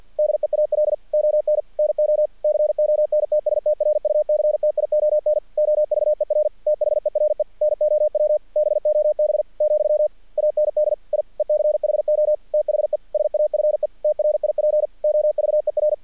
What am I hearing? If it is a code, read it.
Morse code, heard as a keyed tone, about 46 wpm, 610 Hz. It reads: DEAL ON NO CONTSTRUCTION OVER THERE NOW BOB , AND I ECHO THE SAFE TRIP OVER
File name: archived / CRSnet